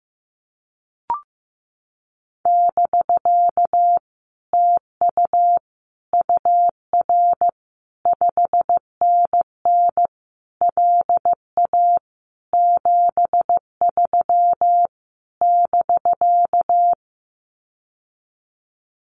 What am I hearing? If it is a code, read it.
Morse code, heard as a keyed tone, about 15 wpm, 700 Hz. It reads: <BK> TU UR 5NN LA 73 <BK>
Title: Common QSO Phrases Encore 15wpm